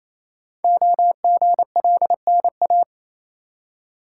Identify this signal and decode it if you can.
Morse code, heard as a keyed tone, about 28 wpm, 700 Hz. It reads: OGLNA
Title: Code Groups - 5 Letters - 28wpm